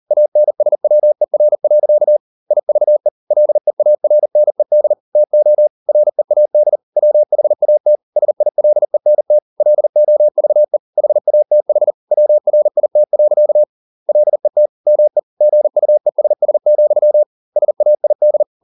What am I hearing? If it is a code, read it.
Morse code, heard as a keyed tone, about 40 wpm, 600 Hz. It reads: ANSWER. IVE LEARNED TO READ WHAT SILENT LOVE HATH WRIT. LET ME GUESS, SAID